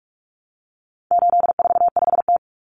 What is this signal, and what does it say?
Morse code, heard as a keyed tone, about 44 wpm, 700 Hz. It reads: 845T
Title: Common Cut Numbers - 4-Digit 45wpm